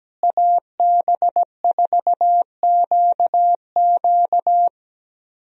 Morse code, heard as a keyed tone, about 17 wpm, 700 Hz. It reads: AB4QQ
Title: USA Call Signs - 5 Characters - Repeated 2x - 17wpm